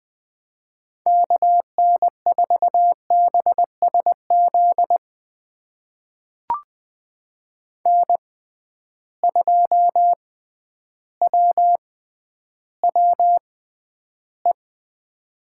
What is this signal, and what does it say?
Morse code, heard as a keyed tone, about 21 wpm, spaced as for 11 wpm, 700 Hz. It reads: KN4BSZ N2WWE
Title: USA Call Signs - 6x Character Spacing - 20wpm